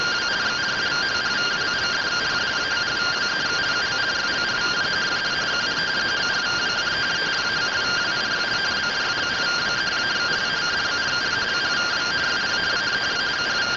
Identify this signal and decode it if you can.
Two-tone teleprinter signal, about 170 Hz shift.